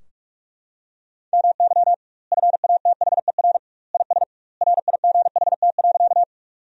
Morse code, heard as a keyed tone, about 41 wpm, 700 Hz. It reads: MY FATHER IS RIGHT.
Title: 2x Word-Spacing - Sentences from Top 300 Words - 45wpm